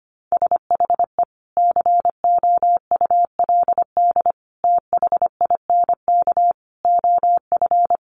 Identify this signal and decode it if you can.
Morse code, heard as a keyed tone, about 25 wpm, 700 Hz. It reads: SHE COULD THINK OF
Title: Sentences from Top 800 Words 25wpm